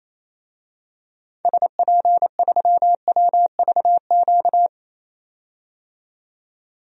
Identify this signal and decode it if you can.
Morse code, heard as a keyed tone, about 28 wpm, 700 Hz. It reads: SP3WVQ
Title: DXCC-100 Call Signs - 6 Characters - 28wpm